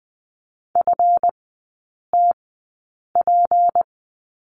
Morse code, heard as a keyed tone, about 20 wpm, spaced as for 8 wpm, 700 Hz. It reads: FTP
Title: 152 Lesson 25e - Letter p - Three Characters 20wpm